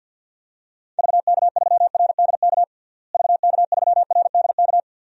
Morse code, heard as a keyed tone, about 47 wpm, 700 Hz. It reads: VK3RDK VK3RDK
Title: DXCC-100 Call Signs - 6 Characters - Repeated 2x - 50wpm